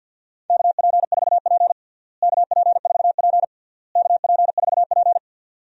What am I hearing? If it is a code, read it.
Morse code, heard as a keyed tone, about 50 wpm, 700 Hz. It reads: KP4P KP4P KP4P
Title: DXCC-100 Call Signs - 4 Characters - Repeated 3x - 50wpm